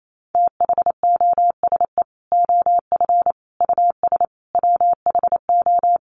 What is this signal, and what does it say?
Morse code, heard as a keyed tone, about 28 wpm, 700 Hz. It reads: THOSE OF US WHO
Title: Sets of 5 Words Encore2 28wpm